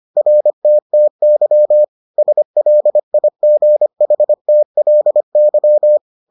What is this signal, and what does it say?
Morse code, heard as a keyed tone, about 25 wpm, 600 Hz. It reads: RTTY SLIGHTLY